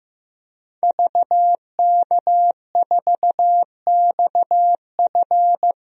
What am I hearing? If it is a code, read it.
Morse code, heard as a keyed tone, about 15 wpm, 700 Hz. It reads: VK4XF
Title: DXCC-100 Call Signs - 5 Characters - Repeated 2x - 15wpm